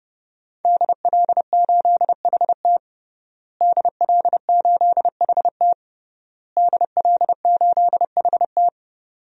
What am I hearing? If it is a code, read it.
Morse code, heard as a keyed tone, about 30 wpm, 700 Hz. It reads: DL8HT DL8HT DL8HT
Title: DXCC-100 Call Signs - 5 Characters - Repeated 3x - 30wpm